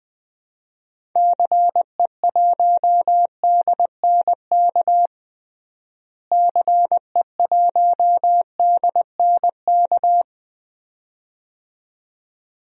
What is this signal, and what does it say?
Morse code, heard as a keyed tone, about 20 wpm, 700 Hz. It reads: CE1DNK CE1DNK
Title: DXCC-100 Call Signs - 6 Characters - Repeated 2x - 20wpm